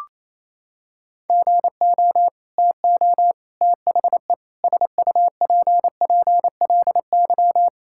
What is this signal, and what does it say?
Morse code, heard as a keyed tone, about 27 wpm, 700 Hz. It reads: GO TO THE SUPPLY
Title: Sentences from Top 1000 Words - Review Part 5 - 28wpm